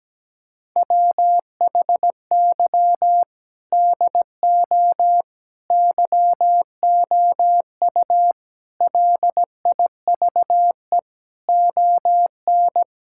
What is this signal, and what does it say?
Morse code, heard as a keyed tone, about 17 wpm, 700 Hz. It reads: WHY DO YOU LIVE ON
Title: Sentences from Top 1000 Words - Review Part 6 - 17wpm